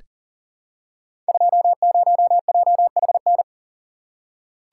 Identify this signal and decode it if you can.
Morse code, heard as a keyed tone, about 39 wpm, 700 Hz. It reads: 20JHN
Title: Code Groups - 5 Letter:Numbers - 40wpm